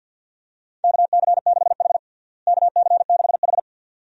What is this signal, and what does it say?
Morse code, heard as a keyed tone, about 50 wpm, 700 Hz. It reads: KC6H KC6H
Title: USA Call Signs - 4 Characters - Repeated 2x - 50wpm